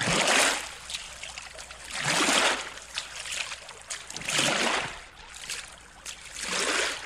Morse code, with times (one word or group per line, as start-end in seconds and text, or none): none